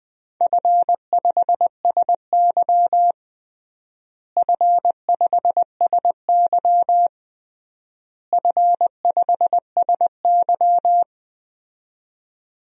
Morse code, heard as a keyed tone, about 20 wpm, 700 Hz. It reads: F5SY F5SY F5SY
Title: DXCC-100 Call Signs - 4 Characters - Repeated 3x - 20wpm